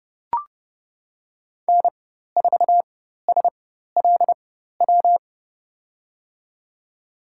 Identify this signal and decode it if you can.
Morse code, heard as a keyed tone, about 30 wpm, spaced as for 14 wpm, 700 Hz. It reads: N4SLW
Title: USA Call Signs - 5 Characters - 4x Character Spacing - 30wpm